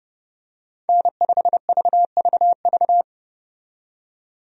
Morse code, heard as a keyed tone, about 30 wpm, 700 Hz. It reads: N5VVV
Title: USA Call Signs - 5 Characters - 30wpm